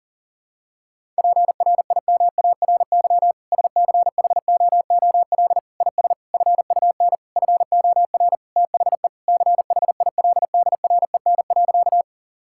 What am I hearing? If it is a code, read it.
Morse code, heard as a keyed tone, about 40 wpm, 700 Hz. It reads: PRIMARY SCHOOL IS FUN FOR THE CHILDREN.
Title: Sentences from Top 1000 Words - Review Part 5 - 40wpm